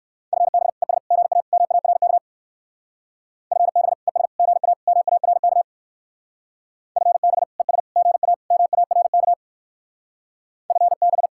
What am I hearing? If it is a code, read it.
Morse code, heard as a keyed tone, about 55 wpm, 700 Hz. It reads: FB ES GA MARK FB ES GA MARK FB ES GA MARK FB
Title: Speed-Racing Common QSO Phrases - 45wpm